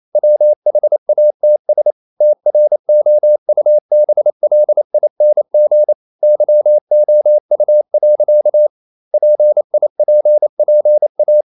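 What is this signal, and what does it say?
Morse code, heard as a keyed tone, about 28 wpm, 600 Hz. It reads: WHATS TROUBLING YOU. PIPPA
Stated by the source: Morse code practice at 28 wpm, text of Ken Follett's novel Never in English, from CW med Gnister.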